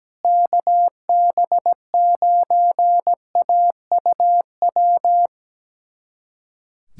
Morse code, heard as a keyed tone, about 17 wpm, 700 Hz. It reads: KB9AUW
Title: USA Callsign DE Callsign 17wpm